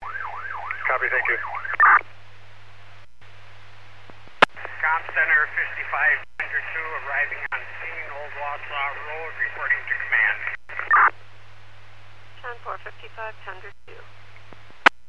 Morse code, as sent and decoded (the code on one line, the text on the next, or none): none